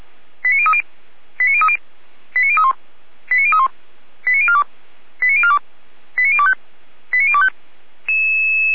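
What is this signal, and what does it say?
Five-tone selective call (ZVEI), heard as quick runs of five tones, tones of about 70 ms. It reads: 89030 89030 89031 89031 89042 89042 89025 89025
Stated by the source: ZVEI five-tone selective call received on SDR.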